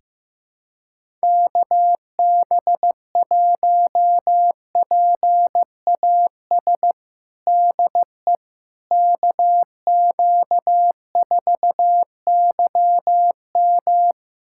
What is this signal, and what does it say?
Morse code, heard as a keyed tone, about 16 wpm, 700 Hz. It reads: KB1PAS DE KQ4YM
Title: USA Callsign DE Callsign 15wpm